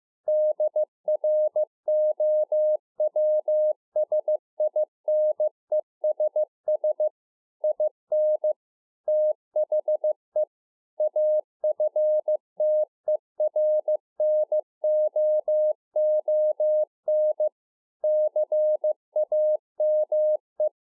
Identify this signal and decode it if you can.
Morse code, heard as a keyed tone, about 15 wpm, 610 Hz. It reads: DROWSINESS IN THE AFTERNOON CAME